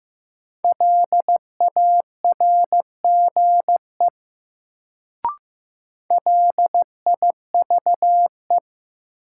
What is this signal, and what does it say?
Morse code, heard as a keyed tone, about 15 wpm, 700 Hz. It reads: LARGE LIVE
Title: Sentences from Top 200 Words - New Word Drill - 15wpm